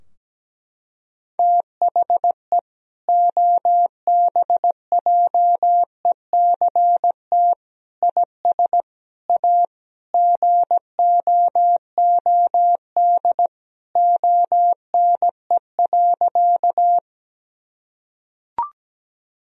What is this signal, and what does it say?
Morse code, heard as a keyed tone, about 17 wpm, 700 Hz. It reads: THE OBJECT IS A GOOD ONE.
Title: Sentences from Top 1000 Words - Review Part 3 - 17wpm